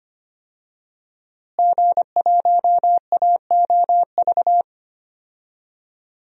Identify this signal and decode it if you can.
Morse code, heard as a keyed tone, about 25 wpm, 700 Hz. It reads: G1AOV
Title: DXCC-100 Call Signs - 5 Characters - Repeated 3x - 25wpm